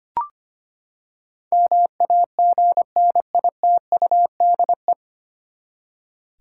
Morse code, heard as a keyed tone, about 25 wpm, 700 Hz. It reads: MAGNITUDE
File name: Nine Letter Words - 25wpm